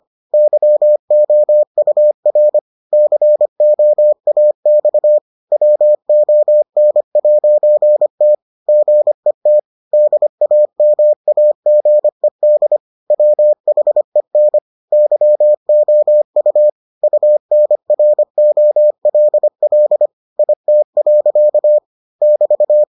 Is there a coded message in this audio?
Yes: Morse code from 0.3 to 22.8 s, heard as a keyed tone, about 25 wpm, 600 Hz. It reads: YOUR COAX WON'T GET DAMAGED WHEN YOU UNROLL IT. <BT>